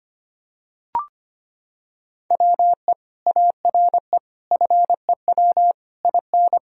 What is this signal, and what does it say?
Morse code, heard as a keyed tone, about 25 wpm, 700 Hz. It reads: WE ARE FEW IN
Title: Sentences from Top 1000 Words - Review Part 1 - 25wpm